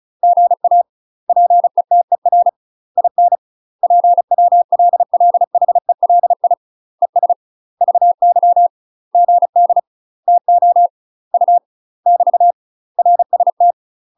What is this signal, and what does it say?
Morse code, heard as a keyed tone, about 34 wpm, 700 Hz. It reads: GA PETER IN PWLLHELI ES VY GD TO U <BT> RST
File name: UK MASTER QSOs TRAINING / 35WPM / UK MASTER QSOs 0003